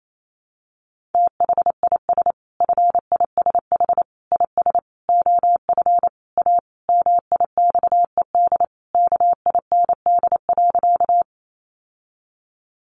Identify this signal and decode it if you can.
Morse code, heard as a keyed tone, about 28 wpm, 700 Hz. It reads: THIS FISH IS OF A MIXED KIND.
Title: Sentences from Top 1500 Words 28wpm